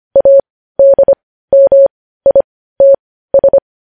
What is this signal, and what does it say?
Morse code, heard as a keyed tone, about 25 wpm, spaced as for 15 wpm, 550 Hz. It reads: ADMITS
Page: news headlines in morse code at 15 wpm